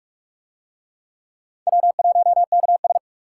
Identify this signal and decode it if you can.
Morse code, heard as a keyed tone, about 39 wpm, 700 Hz. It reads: W1KS